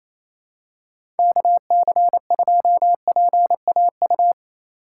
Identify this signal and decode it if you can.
Morse code, heard as a keyed tone, about 28 wpm, 700 Hz. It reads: KC2PAU
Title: USA Call Signs - 6 Characters - 6x Character Spacing - 28wpm